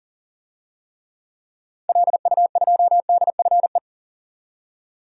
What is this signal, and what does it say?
Morse code, heard as a keyed tone, about 40 wpm, 700 Hz. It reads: LU2DFE
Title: DXCC-100 Call Signs - 6 Characters - 40wpm